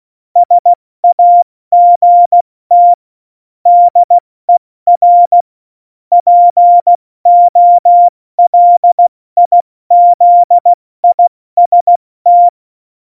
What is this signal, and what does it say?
Morse code, heard as a keyed tone, about 16 wpm, 700 Hz. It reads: SAGT DER POLIZIST
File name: Morsekurs / Witze / Witz 07 Juli 2020, 70 F80